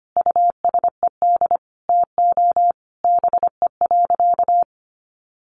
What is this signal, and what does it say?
Morse code, heard as a keyed tone, about 25 wpm, 700 Hz. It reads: USED TO BE.
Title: Sentences from Top 1100 Words 25wpm